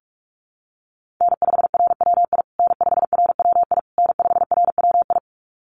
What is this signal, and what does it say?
Morse code, heard as a keyed tone, about 46 wpm, 690 Hz. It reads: N5RWI N5RWI N5RWI
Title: USA Call Signs Repeated 3x 45wpm